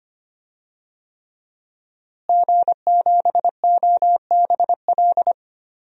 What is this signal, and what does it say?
Morse code, heard as a keyed tone, about 25 wpm, 700 Hz. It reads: G7OBL
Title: DXCC-100 Call Signs - 5 Characters - Repeated 3x - 25wpm